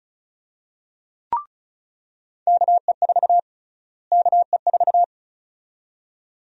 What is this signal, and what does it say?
Morse code, heard as a keyed tone, about 36 wpm, 700 Hz. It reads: KE4 KE4
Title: USA Call Sign Prefixes - Repeated 2x - 35wpm